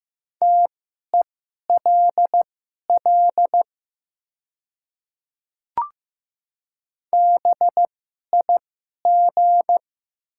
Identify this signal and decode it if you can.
Morse code, heard as a keyed tone, about 15 wpm, spaced as for 11 wpm, 690 Hz. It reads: TELL BIG
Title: 2x Character Spacing - Top 200 Words - 15wpm